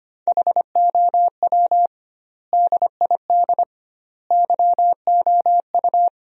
Morse code, heard as a keyed tone, about 25 wpm, 700 Hz. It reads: HOW DID YOU
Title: 2x Word-Spacing - Sentences from Top 100 Words - 25wpm